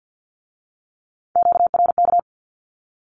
Morse code, heard as a keyed tone, about 50 wpm, 700 Hz. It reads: QRK
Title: Q-Codes with Spoken Meaning - 50wpm